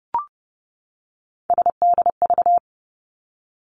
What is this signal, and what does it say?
Morse code, heard as a keyed tone, about 30 wpm, 700 Hz. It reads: SDV
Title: Three Challenging Letter-Number Combinations 30wpm